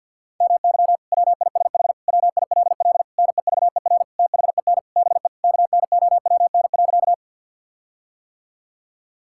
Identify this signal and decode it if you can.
Morse code, heard as a keyed tone, about 50 wpm, 700 Hz. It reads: MY WISH WILL NEVER THEN BE KNOWN.
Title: Sentences from Top 1600 Words 50wpm